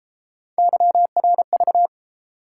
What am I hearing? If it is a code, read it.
Morse code, heard as a keyed tone, about 31 wpm, 700 Hz. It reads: YRV